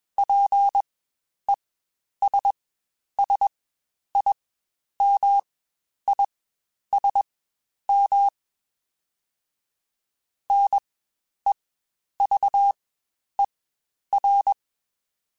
Morse code, heard as a keyed tone, about 21 wpm, spaced as for 10 wpm, 800 Hz. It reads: PESSIMISM NEVER
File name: Quotes Spaced / Quotes Spaced-mp3s / Pessimism sp @20